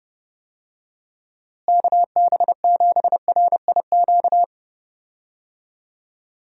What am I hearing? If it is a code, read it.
Morse code, heard as a keyed tone, about 30 wpm, 700 Hz. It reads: KB7RIQ
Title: USA Call Signs - 6 Characters - Repeated 3x - 30wpm